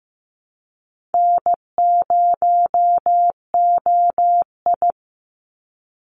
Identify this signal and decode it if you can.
Morse code, heard as a keyed tone, about 15 wpm, 700 Hz. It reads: N0OI
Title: NA SOTA Chaser Callsigns 15wpm